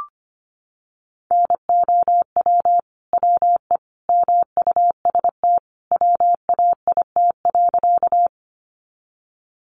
Morse code, heard as a keyed tone, about 25 wpm, 700 Hz. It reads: NOW WE MUST WAIT.
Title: Sentences from Top 500 Words 25wpm